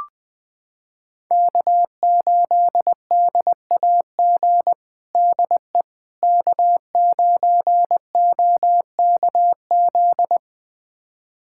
Morse code, heard as a keyed tone, about 21 wpm, 700 Hz. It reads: K8DAG DE K9OKZ
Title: USA Callsign DE Callsign - 5 Characters - 20wpm